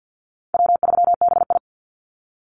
Morse code, heard as a keyed tone, about 50 wpm, 700 Hz. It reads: W3BI
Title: Easy Call Signs Repeated 2x - 50wpm